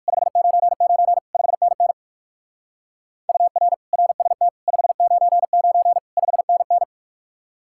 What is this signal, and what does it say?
Morse code, heard as a keyed tone, about 47 wpm, 700 Hz. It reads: 599 5NN UR RST 599 5NN
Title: Speed-Racing Common QSO Phrases - 45wpm